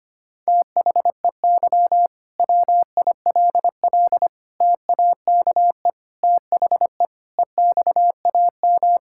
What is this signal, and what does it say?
Morse code, heard as a keyed tone, about 25 wpm, 700 Hz. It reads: THEY WILL TAKE THE EXAM
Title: Sentences from Top 1000 Words - Review Part 5 - 25wpm